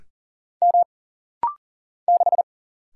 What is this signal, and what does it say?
Morse code, heard as a keyed tone, about 40 wpm, 700 Hz. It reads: M 6
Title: Single Letter-Number - Rapid-Fire 40wpm